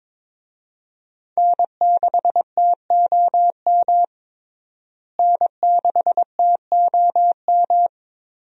Morse code, heard as a keyed tone, about 22 wpm, 700 Hz. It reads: N6TOM N6TOM
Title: USA Call Signs - 5 Characters - Repeated 2x - 22wpm